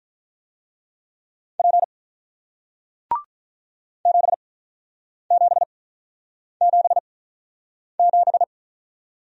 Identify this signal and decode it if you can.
Morse code, heard as a keyed tone, about 48 wpm, spaced as for 9 wpm, 700 Hz. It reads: P 7777